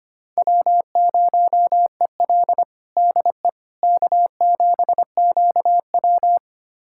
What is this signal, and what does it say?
Morse code, heard as a keyed tone, about 25 wpm, 700 Hz. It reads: W0EL DE K7QW
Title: USA Callsign DE Callsign - 4 Characters - 25wpm